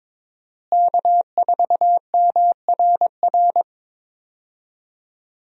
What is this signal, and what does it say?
Morse code, heard as a keyed tone, about 22 wpm, 700 Hz. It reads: K4MRR
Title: USA Call Signs - 5 Characters - Repeated 3x - 22wpm